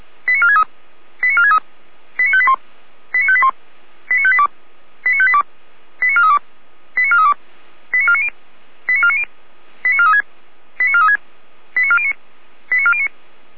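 Five-tone selective call (ZVEI), heard as quick runs of five tones, tones of about 70 ms. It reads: 89463 89463 89581 89581 89582 89582 89432 89432 89490 89490 89436 89436 89409 89409
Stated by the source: ZVEI five-tone selective call received by SDR, feuerwehr-orlamuende fire brigade server.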